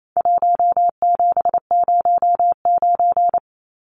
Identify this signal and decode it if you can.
Morse code, heard as a keyed tone, about 28 wpm, 700 Hz. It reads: 1709
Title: Numbers - Contest Style - 4-Digit 28wpm